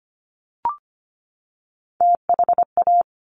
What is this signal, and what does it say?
Morse code, heard as a keyed tone, about 25 wpm, 700 Hz. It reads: THA
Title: Top Three Letter Combinations in Words 25wpm